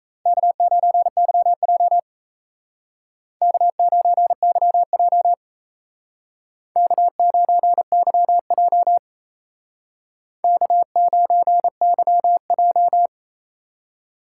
Morse code, heard as a keyed tone, about 35 wpm, 700 Hz. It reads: K9YJ K9YJ K9YJ K9YJ